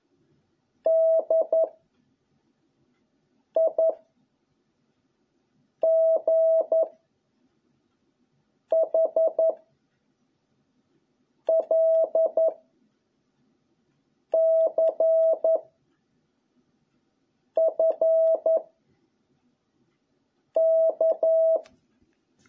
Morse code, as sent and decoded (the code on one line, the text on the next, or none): -.. / .. / --. / .... / .-.. / -.-. / ..-. / -.-
D I G H L C F K